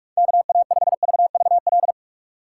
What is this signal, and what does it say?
Morse code, heard as a keyed tone, about 46 wpm, 700 Hz. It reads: KA5VVL